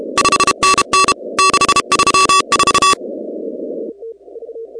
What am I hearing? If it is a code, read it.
Morse code, heard as a keyed tone, about 32 wpm, 1140 Hz, with noise. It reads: 5NN 634
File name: contests / 2019-cw-wpx-kt5j-k5pi- -2330 0-16